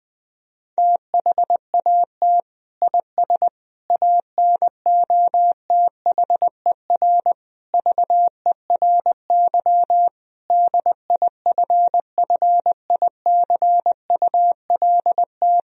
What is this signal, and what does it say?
Morse code, heard as a keyed tone, about 20 wpm, 700 Hz. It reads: THAT IS ANOTHER VERY DIFFICULT
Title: Sentences from Top 1000 Words - Review Part 2 - 20wpm